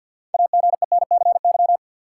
Morse code, heard as a keyed tone, about 53 wpm, 700 Hz. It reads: AGENCY
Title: Sentences from Top 1300 Words - New Word Drill - 50wpm